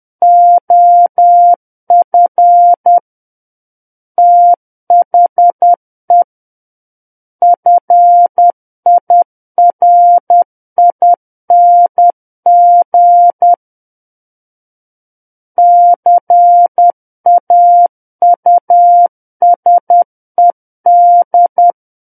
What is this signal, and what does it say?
Morse code, heard as a keyed tone, about 10 wpm, 700 Hz. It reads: OF THE FIRING CAUSED
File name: Book One / 01-Chapter 1 - 10 WPM